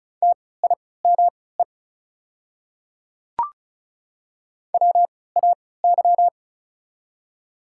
Morse code, heard as a keyed tone, about 36 wpm, spaced as for 20 wpm, 700 Hz. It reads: TIME WAY
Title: 3x Character Spacing - Top 100 Words - 35wpm